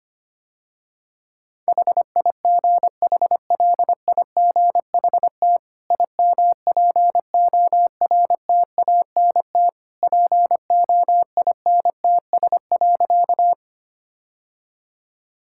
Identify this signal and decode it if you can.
Morse code, heard as a keyed tone, about 25 wpm, 700 Hz. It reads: HIGHLIGHT IMPORTANT POINTS.
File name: Easy Sentences with Hints - 25wpm